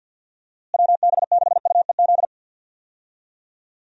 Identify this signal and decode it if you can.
Morse code, heard as a keyed tone, about 49 wpm, 700 Hz. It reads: WB6UEZ
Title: USA Call Signs - 6 Characters - Repeated 2x - 50wpm